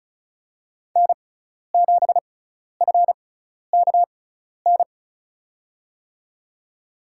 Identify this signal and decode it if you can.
Morse code, heard as a keyed tone, about 34 wpm, spaced as for 12 wpm, 700 Hz. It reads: N7FKN
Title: USA Call Signs - 6x Character Spacing - 35wpm